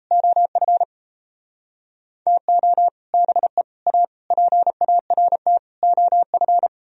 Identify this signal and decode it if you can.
Morse code, heard as a keyed tone, about 37 wpm, 700 Hz. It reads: OF TO BE A PART OF